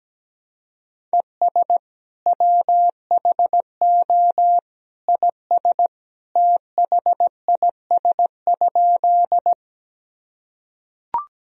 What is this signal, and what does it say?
Morse code, heard as a keyed tone, about 17 wpm, 700 Hz. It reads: ES WHO IS THIS?
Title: Sentences from Top 100 Words and Abbreviations 17wpm